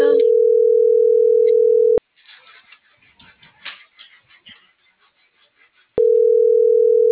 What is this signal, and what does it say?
Ringback tone (North America), 440 + 480 Hz.